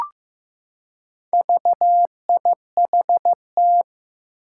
Morse code, heard as a keyed tone, about 15 wpm, 700 Hz. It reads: VIHT
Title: Four Challenging Letter-Number Combinations 15wpm